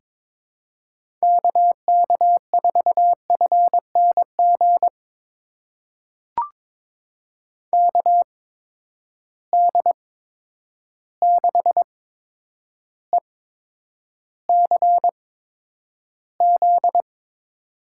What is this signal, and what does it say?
Morse code, heard as a keyed tone, about 22 wpm, spaced as for 9 wpm, 700 Hz. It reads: KK4FNG KD6ECZ
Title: USA Call Signs - 6 Characters - 8x Character Spacing - 22wpm